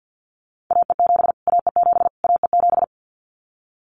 Morse code, heard as a keyed tone, about 50 wpm, 700 Hz. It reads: AE7 AE7 AE7
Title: Callsign Prefixes for Top 25 DXCC - Repeated 3x - 50wpm